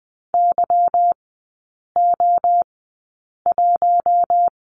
Morse code, heard as a keyed tone, about 20 wpm, spaced as for 8 wpm, 690 Hz. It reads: YO1
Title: 145 Lesson 24e - Letter y - Three Characters 20wpm